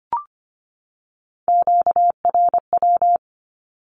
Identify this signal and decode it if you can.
Morse code, heard as a keyed tone, about 25 wpm, 700 Hz. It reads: QRW
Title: Q-Codes 25wpm